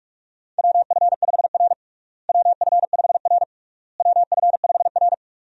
Morse code, heard as a keyed tone, about 46 wpm, 700 Hz. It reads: WF5R WF5R WF5R
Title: USA Call Signs - 4 Characters - Repeated 3x - 45wpm